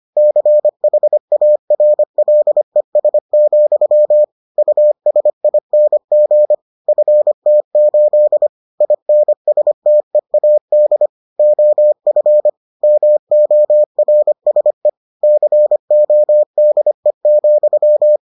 Morse code, heard as a keyed tone, about 25 wpm, 600 Hz. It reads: CHARLES, USING FT8 INSTEAD OF MORSE CODE,